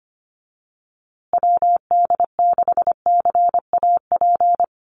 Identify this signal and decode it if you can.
Morse code, heard as a keyed tone, about 25 wpm, 700 Hz. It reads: WD6CAP
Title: USA Call Signs Repeated 2x 25wpm